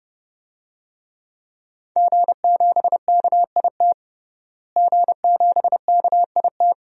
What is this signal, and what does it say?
Morse code, heard as a keyed tone, about 29 wpm, 700 Hz. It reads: G7KIT G7KIT
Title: DXCC-100 Call Signs - 5 Characters - Repeated 2x - 30wpm